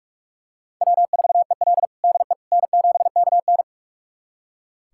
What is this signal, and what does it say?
Morse code, heard as a keyed tone, about 46 wpm, 700 Hz. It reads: W4EL DE N7KN
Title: USA Callsign DE Callsign - 4 Characters - 45wpm